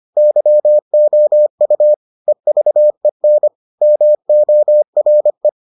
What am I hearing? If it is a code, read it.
Morse code, heard as a keyed tone, about 25 wpm, 600 Hz. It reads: YOU EVEN MORE